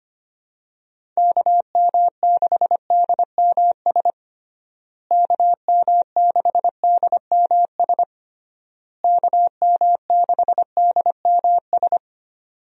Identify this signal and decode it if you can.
Morse code, heard as a keyed tone, about 25 wpm, 700 Hz. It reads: KM6DMS KM6DMS KM6DMS
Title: USA Call Signs - 6 Characters - Repeated 3x - 25wpm